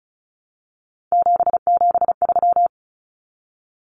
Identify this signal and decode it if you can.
Morse code, heard as a keyed tone, about 34 wpm, 700 Hz. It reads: 773
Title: Common Cut Numbers - 3-Digit 35wpm